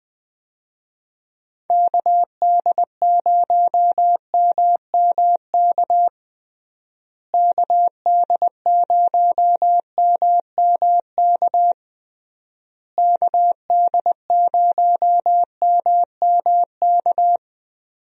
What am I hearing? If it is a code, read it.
Morse code, heard as a keyed tone, about 20 wpm, 700 Hz. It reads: KD0MMK KD0MMK KD0MMK
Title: USA Call Signs - 6 Characters - Repeated 3x - 20wpm